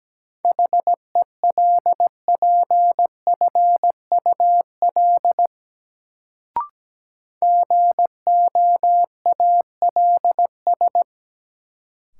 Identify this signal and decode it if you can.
Morse code, heard as a keyed tone, about 17 wpm, 700 Hz. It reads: HELPFUL GOALS
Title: Sentences from Top 1000 Words - Review Part 6 - New Word Drill - 17wpm